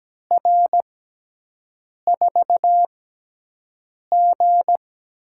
Morse code, heard as a keyed tone, about 17 wpm, spaced as for 6 wpm, 700 Hz. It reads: R4G
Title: USA Call Signs - 5 Characters - 6x Character Spacing - 17wpm